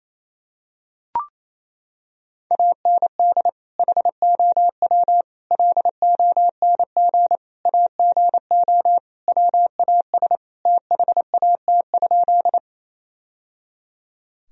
Morse code, heard as a keyed tone, about 28 wpm, 700 Hz. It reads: AND HOW LONG AGO WAS THAT?
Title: Sentences from Top 1000 Words - Review Part 3 - 28wpm